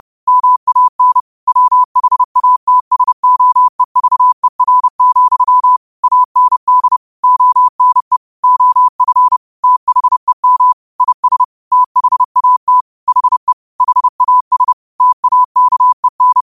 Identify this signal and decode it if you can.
Morse code, heard as a keyed tone, about 30 wpm, 1000 Hz. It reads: MAN WHATSOEVER, AND ONE OF THEM IS THAT HE HAS TAKEN